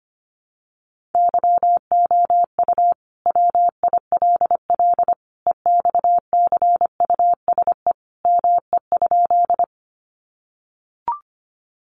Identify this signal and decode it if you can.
Morse code, heard as a keyed tone, about 25 wpm, 700 Hz. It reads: YOU WILL EXCUSE ME?
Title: Sentences from Top 700 Words 25wpm